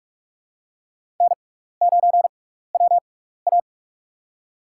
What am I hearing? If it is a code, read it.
Morse code, heard as a keyed tone, about 46 wpm, spaced as for 16 wpm, 700 Hz. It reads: N9WA
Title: USA Call Signs - 4 Characters - 6x Character Spacing - 45wpm